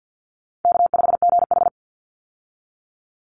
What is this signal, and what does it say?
Morse code, heard as a keyed tone, about 48 wpm, 700 Hz. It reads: K5GH
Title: Call Signs Repeated 3x - 50wpm